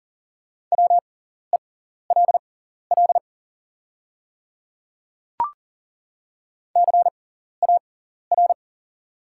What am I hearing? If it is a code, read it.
Morse code, heard as a keyed tone, about 40 wpm, spaced as for 14 wpm, 700 Hz. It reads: WELL CAR
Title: 6x Character Spacing - Top 300 Words - 40wpm